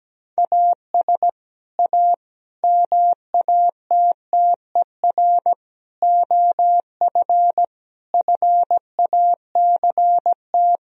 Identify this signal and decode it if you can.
Morse code, heard as a keyed tone, about 17 wpm, 700 Hz. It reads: AS A MATTER OF FACT
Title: Sentences from Top 1000 Words - Review Part 3 - 17wpm